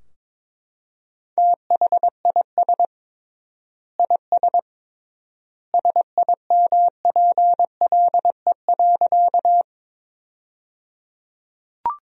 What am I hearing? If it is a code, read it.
Morse code, heard as a keyed tone, about 22 wpm, 700 Hz. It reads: THIS IS SIMPLE.
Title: 3x Word-Spacing - Sentences from Top 400 Words - 22wpm